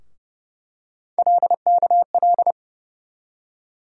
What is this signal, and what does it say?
Morse code, heard as a keyed tone, about 30 wpm, 700 Hz. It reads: LKL
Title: USA Call Sign Postfixes - 3 Letters 30wpm